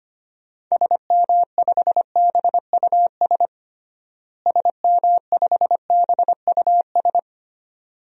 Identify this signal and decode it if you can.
Morse code, heard as a keyed tone, about 25 wpm, 700 Hz. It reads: SM5BUS SM5BUS
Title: DXCC-100 Call Signs - 6 Characters - Repeated 2x - 25wpm